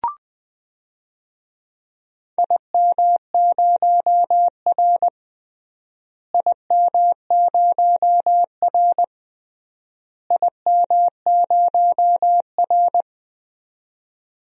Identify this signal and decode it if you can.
Morse code, heard as a keyed tone, about 21 wpm, 700 Hz. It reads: IM0R IM0R IM0R
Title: DXCC-100 Call Signs - 4 Characters - Repeated 3x - 20wpm